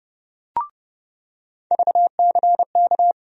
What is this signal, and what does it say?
Morse code, heard as a keyed tone, about 31 wpm, 700 Hz. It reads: VCK
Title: USA Call Sign Postfixes - 3 Letters 30wpm